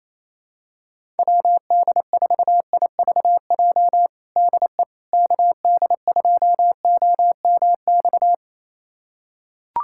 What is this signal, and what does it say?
Morse code, heard as a keyed tone, about 28 wpm, 700 Hz. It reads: WD4IVJ DE KD2OMX
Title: USA Callsign DE Callsign 28wpm